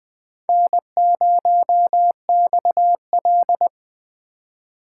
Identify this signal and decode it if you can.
Morse code, heard as a keyed tone, about 22 wpm, 700 Hz. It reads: N0XL